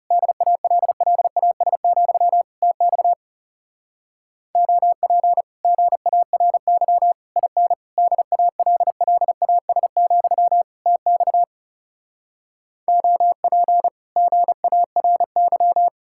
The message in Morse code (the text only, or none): DALLAS, TX OP GARY IN DALLAS, TX OP GARY